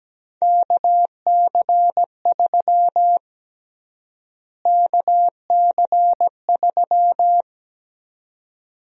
Morse code, heard as a keyed tone, about 17 wpm, 700 Hz. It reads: KC3 KC3
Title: USA Call Sign Prefixes - Repeated 2x - 17wpm